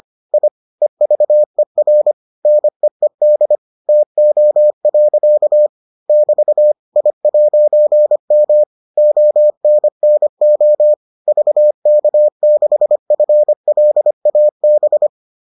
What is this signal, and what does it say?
Morse code, heard as a keyed tone, about 25 wpm, 600 Hz. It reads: I EVER NEED TO. <BT> I'M ONNO VK6FLAB